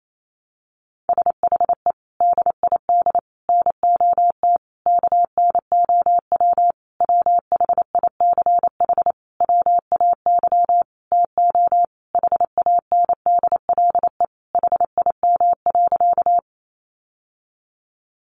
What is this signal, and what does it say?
Morse code, heard as a keyed tone, about 28 wpm, 700 Hz. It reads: SHE DID NOT KNOW WHICH WAY TO HANDLE HIM.
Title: Sentences from Top 1100 Words 28wpm